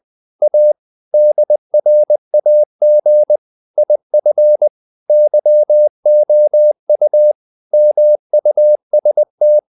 Morse code, heard as a keyed tone, about 20 wpm, 600 Hz. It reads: A DRAG IF YOU MUST
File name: the old man and the sea part 2 20wpm